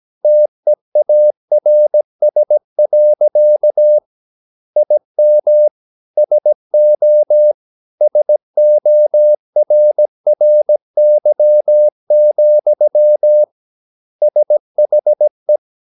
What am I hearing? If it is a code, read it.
Morse code, heard as a keyed tone, about 17 wpm, 600 Hz. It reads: TEARS. IM SO SORRY, SHE
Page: Romanen Never af Ken Follet på engelsk.